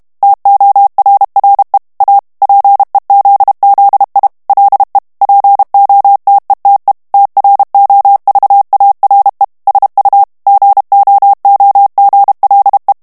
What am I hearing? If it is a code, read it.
Morse code, heard as a keyed tone, about 32 wpm, 800 Hz. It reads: TORRE A PEZZI LE POTETE TROVARE SU GOOGLE
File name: ITALIEN texte clair 1 vitesse 30 mots minute